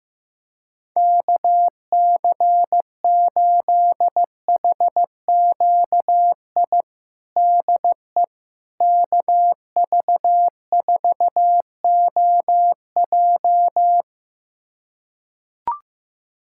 Morse code, heard as a keyed tone, about 15 wpm, 700 Hz. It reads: KC8HQI DE KV4OJ
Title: USA Callsign DE Callsign 15wpm